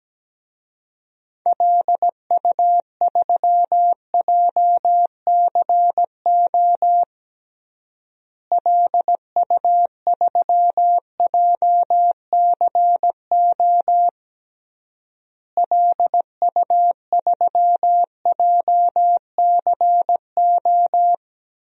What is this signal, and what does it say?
Morse code, heard as a keyed tone, about 17 wpm, 700 Hz. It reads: LU3JCO LU3JCO LU3JCO
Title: DXCC-100 Call Signs - 6 Characters - Repeated 3x - 17wpm